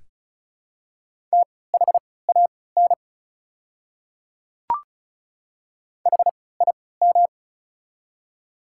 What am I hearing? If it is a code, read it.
Morse code, heard as a keyed tone, about 34 wpm, spaced as for 20 wpm, 700 Hz. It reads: THAN HIM
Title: 3x Character Spacing - Top 100 Words - 35wpm